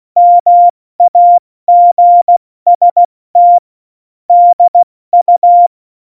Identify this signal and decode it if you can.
Morse code, heard as a keyed tone, about 16 wpm, 700 Hz. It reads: MAGST DU
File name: Morsekurs / Witze / Witz 15. Mai 2018, 70 F80